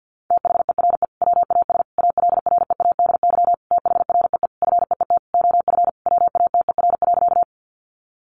Morse code, heard as a keyed tone, about 50 wpm, 700 Hz. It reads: THERE WAS ALREADY THREE FEET OF WATER.
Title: Sentences from Top 700 Words 50wpm